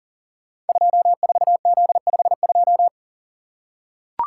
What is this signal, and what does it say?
Morse code, heard as a keyed tone, about 38 wpm, 690 Hz. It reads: 24Z52
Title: Code Groups - 5 Letter:Numbers - 40wpm